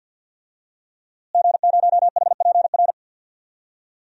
Morse code, heard as a keyed tone, about 52 wpm, 700 Hz. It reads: G0HPR
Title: DXCC-100 Call Signs - 5 Characters - Repeated 3x - 50wpm